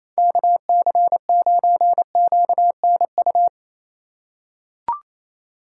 Morse code, heard as a keyed tone, about 28 wpm, 700 Hz. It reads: KC9QNU
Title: USA Call Signs - 4x Character Spacing - 28wpm